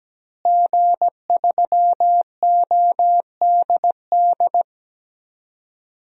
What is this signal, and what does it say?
Morse code, heard as a keyed tone, about 17 wpm, 700 Hz. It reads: G3ODD
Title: DXCC-100 Call Signs - 5 Characters - Repeated 3x - 17wpm